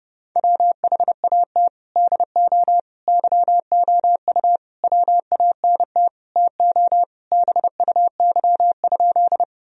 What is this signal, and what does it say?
Morse code, heard as a keyed tone, about 32 wpm, 700 Hz. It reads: WHAT DO YOU WANT TO BUY?
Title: Sentences from Top 1000 Words - Review Part 4 - 30wpm